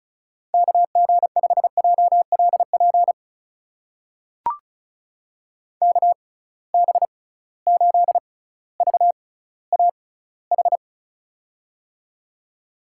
Morse code, heard as a keyed tone, about 34 wpm, spaced as for 18 wpm, 700 Hz. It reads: KG5JLP KB8VAH
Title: USA Call Signs - 6 Characters - 6x Character Spacing - 35wpm